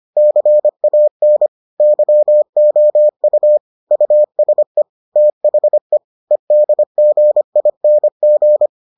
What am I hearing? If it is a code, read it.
Morse code, heard as a keyed tone, about 25 wpm, 600 Hz. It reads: CAN YOU USE THE EDGING